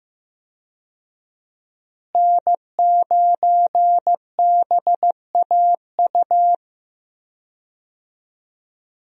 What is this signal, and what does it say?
Morse code, heard as a keyed tone, about 15 wpm, 700 Hz. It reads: N9BAU
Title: USA Call Signs - 5 Characters - Repeated 3x - 15wpm